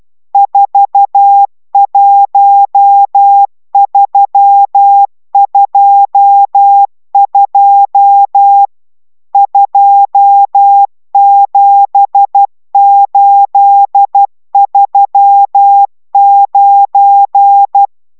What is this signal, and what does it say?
Morse code, heard as a keyed tone, about 12 wpm, 800 Hz. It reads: 41322 27839